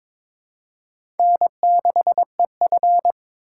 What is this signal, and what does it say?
Morse code, heard as a keyed tone, about 22 wpm, 700 Hz. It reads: N6EF
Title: USA Call Signs - 4 Characters - 3x Character Spacing - 22wpm